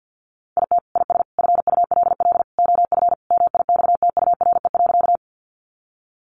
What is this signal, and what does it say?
Morse code, heard as a keyed tone, about 50 wpm, 700 Hz. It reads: IT IS FULL OF MIXTURE.
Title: Sentences from Top 1500 Words 50wpm